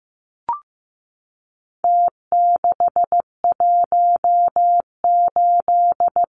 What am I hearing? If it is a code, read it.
Morse code, heard as a keyed tone, about 15 wpm, 700 Hz. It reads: T618
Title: Numbers - Contest Style - 4-Digit 15wpm